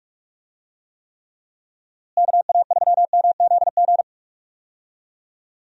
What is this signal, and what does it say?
Morse code, heard as a keyed tone, about 42 wpm, 700 Hz. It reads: KA3MZG
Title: USA Call Signs - 6 Characters - Repeated 3x - 45wpm